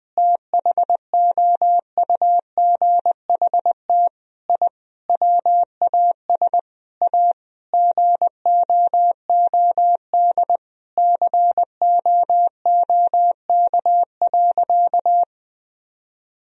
Morse code, heard as a keyed tone, about 20 wpm, 700 Hz. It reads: THOUGHT I WAS A GOOD COOK.
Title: Sentences from Top 1000 Words - Review Part 4 - 20wpm